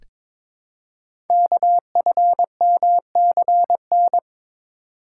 Morse code, heard as a keyed tone, about 22 wpm, 700 Hz. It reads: KFMCN
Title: Code Groups - 5 Letters - 22wpm